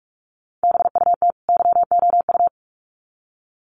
Morse code, heard as a keyed tone, about 44 wpm, 700 Hz. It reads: BUT YOU
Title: Sets of 2 Words 45wpm